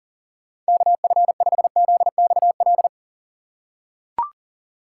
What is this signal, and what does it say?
Morse code, heard as a keyed tone, about 40 wpm, 700 Hz. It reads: KF5ZXL
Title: USA Call Signs - 6 Characters - 4x Character Spacing - 40wpm